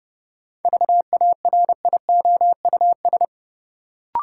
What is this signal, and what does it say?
Morse code, heard as a keyed tone, about 29 wpm, 700 Hz. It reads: VARIOUS
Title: Sentences from Top 1000 Words - Review Part 4 - New Word Drill - 30wpm